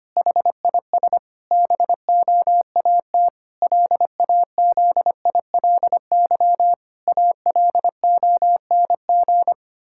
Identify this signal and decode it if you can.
Morse code, heard as a keyed tone, about 25 wpm, 700 Hz. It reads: HIS BOAT LAZILY ALONG
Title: Pangram Challenge v2 - 25wpm